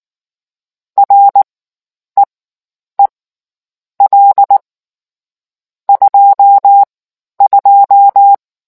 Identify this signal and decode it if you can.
Morse code, heard as a keyed tone, about 19 wpm, spaced as for 9 wpm, 800 Hz. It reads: REEL 22